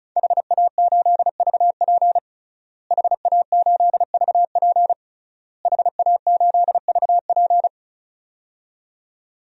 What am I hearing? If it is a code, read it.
Morse code, heard as a keyed tone, about 36 wpm, 700 Hz. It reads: HA8VP HA8VP HA8VP
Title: DXCC-100 Call Signs - 5 Characters - Repeated 3x - 35wpm